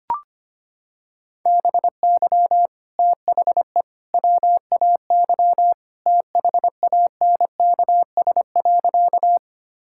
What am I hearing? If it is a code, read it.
Morse code, heard as a keyed tone, about 25 wpm, 700 Hz. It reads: BY THE WAY THANKS.
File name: practice / 25wpm / Sentences with 3-Word Phrases - 25wpm